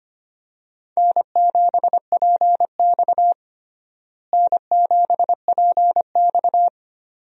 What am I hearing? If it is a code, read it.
Morse code, heard as a keyed tone, about 25 wpm, 700 Hz. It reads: N7PX N7PX
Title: USA Call Signs - 4 Characters - Repeated 2x - 25wpm